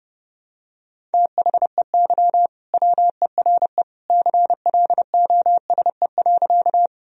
Morse code, heard as a keyed tone, about 30 wpm, 700 Hz. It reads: THEY WERE CLOSE.
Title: Sentences from Top 1000 Words - Review Part 1 - 30wpm